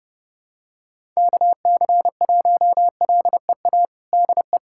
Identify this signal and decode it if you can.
Morse code, heard as a keyed tone, about 32 wpm, 700 Hz. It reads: KC1LEA DE
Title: USA Callsign DE Callsign 30wpm